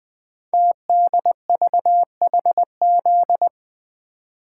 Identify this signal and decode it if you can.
Morse code, heard as a keyed tone, about 20 wpm, 700 Hz. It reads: TDVHZ
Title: Code Groups - 5 Letters - 20wpm